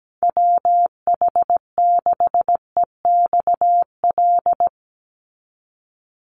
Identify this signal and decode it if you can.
Morse code, heard as a keyed tone, about 17 wpm, 700 Hz. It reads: WH6EXL
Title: USA Call Signs 17wpm